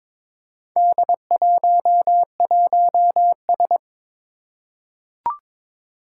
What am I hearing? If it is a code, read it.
Morse code, heard as a keyed tone, about 23 wpm, 700 Hz. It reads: D11S
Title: Four Challenging Letter-Number Combinations 22wpm